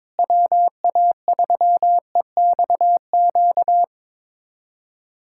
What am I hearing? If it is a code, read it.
Morse code, heard as a keyed tone, about 22 wpm, 700 Hz. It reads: WA3EXQ
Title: USA Callsign DE Callsign - 6 Characters - 22wpm